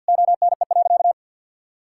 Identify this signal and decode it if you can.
Morse code, heard as a keyed tone, about 52 wpm, 700 Hz. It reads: ONE.
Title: Sentences from Top 1000 Words - Review Part 5 - 50wpm